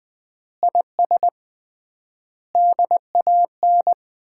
Morse code, heard as a keyed tone, about 20 wpm, 700 Hz. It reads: IS DAN
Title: Common QSO Phrases - 3x Word-Spacing - 20wpm